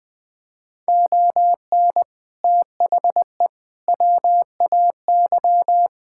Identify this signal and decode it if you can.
Morse code, heard as a keyed tone, about 20 wpm, 700 Hz. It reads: ON THE WAY
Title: Build A Phrase - 4-Word Phrases - 20wpm